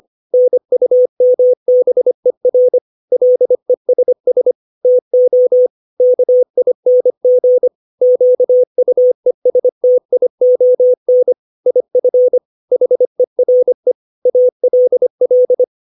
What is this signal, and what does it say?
Morse code, heard as a keyed tone, about 25 wpm, 500 Hz. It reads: NUMBER LESS TO KING QUESTION IF HERE ALL